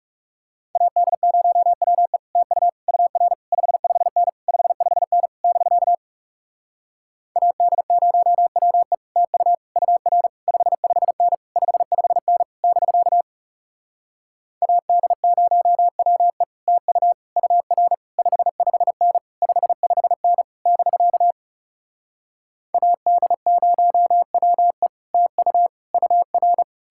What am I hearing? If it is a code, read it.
Morse code, heard as a keyed tone, about 37 wpm, 700 Hz. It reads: AD0WE TU UR 55N 55N <BK> AD0WE TU UR 55N 55N <BK> AD0WE TU UR 55N 55N <BK> AD0WE TU UR
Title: Speed-Racing Common QSO Phrases - 30wpm